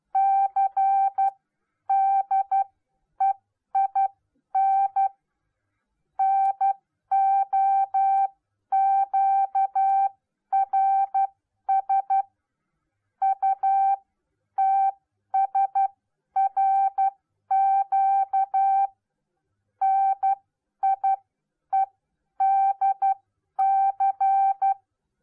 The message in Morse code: CDEIN NOQRS UTSRQ NIEDC

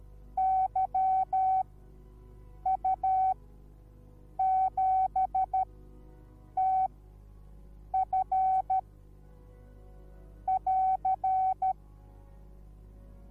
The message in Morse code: YU7TF <AR>

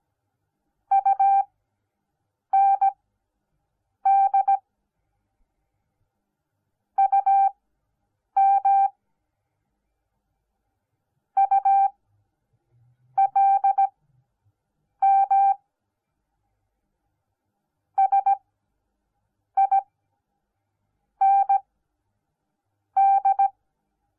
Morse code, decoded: UND UM ULM SIND

